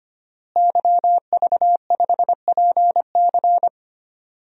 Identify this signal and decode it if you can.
Morse code, heard as a keyed tone, about 25 wpm, 700 Hz. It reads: YV5PC
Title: DXCC-100 Call Signs - 5 Characters - Repeated 3x - 25wpm